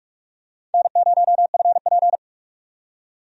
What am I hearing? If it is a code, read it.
Morse code, heard as a keyed tone, about 46 wpm, 700 Hz. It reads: N0FP
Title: USA Call Signs - 4 Characters - 45wpm